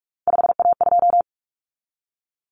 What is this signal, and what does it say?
Morse code, heard as a keyed tone, about 44 wpm, 700 Hz. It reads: 5A2
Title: Common Cut Numbers - 3-Digit 45wpm